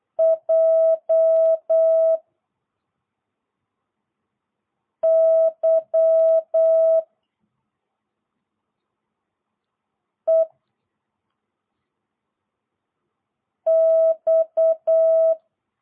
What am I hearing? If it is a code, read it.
Morse code, heard as a keyed tone, about 8 wpm, 650 Hz. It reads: J Y E X